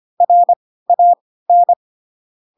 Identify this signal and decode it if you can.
Morse code, heard as a keyed tone, about 25 wpm, spaced as for 16 wpm, 700 Hz. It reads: RAN
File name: Pinocchio / Pinocchio ch3 10 25 ik1hge